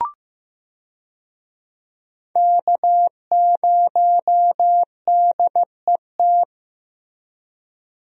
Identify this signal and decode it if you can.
Morse code, heard as a keyed tone, about 15 wpm, 700 Hz. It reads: K0DET
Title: USA Call Signs - 5 Characters - Repeated 3x - 15wpm